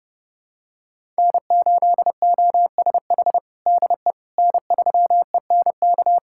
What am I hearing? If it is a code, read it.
Morse code, heard as a keyed tone, about 32 wpm, 700 Hz. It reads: N8OSH DE N3ENK
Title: USA Callsign DE Callsign - 5 Characters - 30wpm